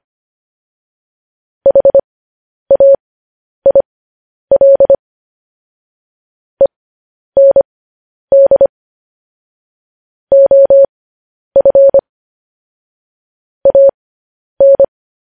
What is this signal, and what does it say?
Morse code, heard as a keyed tone, about 25 wpm, spaced as for 10 wpm, 550 Hz. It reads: HAIL END OF AN
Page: news headlines in morse code at 10 wpm